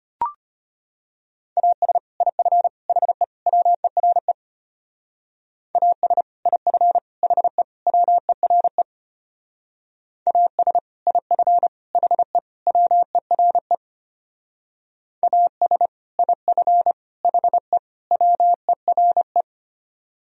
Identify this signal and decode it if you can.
Morse code, heard as a keyed tone, about 29 wpm, 700 Hz. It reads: AS IF HE WERE AS IF HE WERE AS IF HE WERE AS IF HE WERE
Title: Speed-Racing - Sets of 4 Words 25wpm